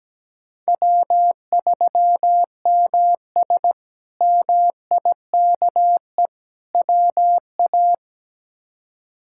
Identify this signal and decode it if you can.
Morse code, heard as a keyed tone, about 17 wpm, 700 Hz. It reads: W3MS MIKE WA
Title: K1USN SST QSO Practice - 17wpm